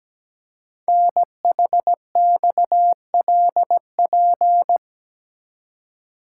Morse code, heard as a keyed tone, about 17 wpm, 700 Hz. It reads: NHXLP
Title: Code Groups - 5 Letter:Numbers - 17wpm